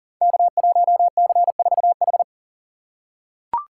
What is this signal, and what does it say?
Morse code, heard as a keyed tone, about 39 wpm, 700 Hz. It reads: K1C4H
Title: Code Groups - 5 Letter:Numbers - 40wpm